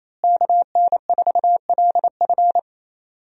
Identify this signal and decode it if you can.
Morse code, heard as a keyed tone, about 27 wpm, 700 Hz. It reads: KN4LF
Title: USA Call Signs - 5 Characters - Repeated 2x - 28wpm